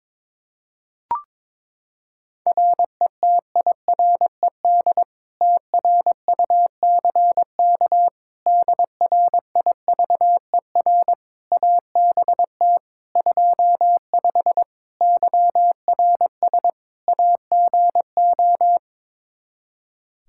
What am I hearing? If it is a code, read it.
Morse code, heard as a keyed tone, about 22 wpm, 700 Hz. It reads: RETIRED TRUCK DRIVER ABT 25 YRS AGO
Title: Practice QSO - Exchange 3 - More Info - 22wpm